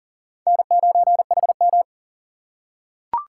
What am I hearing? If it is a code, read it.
Morse code, heard as a keyed tone, about 40 wpm, 700 Hz. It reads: N9HM
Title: USA Call Signs - 4 Characters - 6x Character Spacing - 40wpm